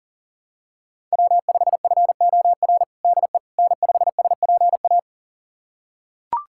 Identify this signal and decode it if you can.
Morse code, heard as a keyed tone, about 40 wpm, 700 Hz. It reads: W5FOR DE N5SPA
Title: USA Callsign DE Callsign - 5 Characters - 40wpm